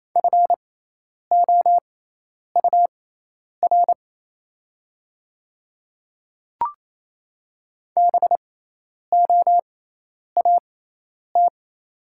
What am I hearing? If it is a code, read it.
Morse code, heard as a keyed tone, about 28 wpm, spaced as for 10 wpm, 690 Hz. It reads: FOUR BOAT
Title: 6x Character Spacing - Top 500 Words - 28wpm